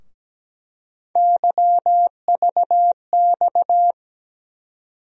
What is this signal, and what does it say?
Morse code, heard as a keyed tone, about 17 wpm, 700 Hz. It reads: YVX
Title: USA Call Sign Postfixes - 3 Letters 17wpm